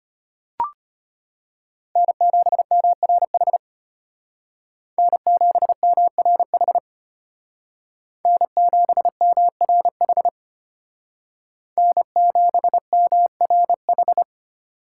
Morse code, heard as a keyed tone, about 33 wpm, 700 Hz. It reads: N7MRH N7MRH N7MRH N7MRH